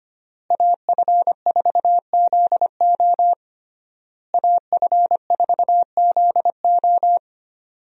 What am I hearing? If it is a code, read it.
Morse code, heard as a keyed tone, about 25 wpm, 700 Hz. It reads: AF4ZO AF4ZO
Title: USA Call Signs - 5 Characters - Repeated 3x - 25wpm